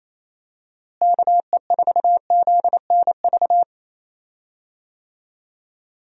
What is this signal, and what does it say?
Morse code, heard as a keyed tone, about 28 wpm, 700 Hz. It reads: KE4ZNV
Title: USA Call Signs - 6 Characters - Repeated 3x - 28wpm